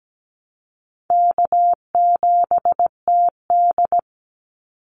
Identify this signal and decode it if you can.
Morse code, heard as a keyed tone, about 17 wpm, 700 Hz. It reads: K7TD
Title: Easy Call Signs Repeated 3x - 17wpm